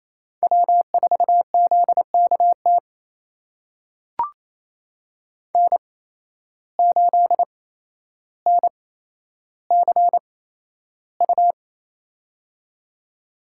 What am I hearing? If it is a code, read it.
Morse code, heard as a keyed tone, about 28 wpm, spaced as for 12 wpm, 700 Hz. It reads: W4ZKT N8NCU
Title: USA Call Signs - 5 Characters - 8x Character Spacing - 28wpm